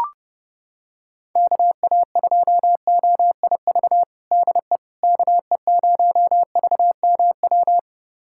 Morse code, heard as a keyed tone, about 30 wpm, 700 Hz. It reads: KA2OIV DE KE0VMW
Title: USA Callsign DE Callsign - 6 Characters - 30wpm